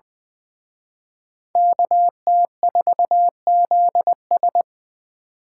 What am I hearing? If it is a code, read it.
Morse code, heard as a keyed tone, about 23 wpm, 700 Hz. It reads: KT4ZS
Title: Speed Racing - USA 2x2 Call Signs 20wpm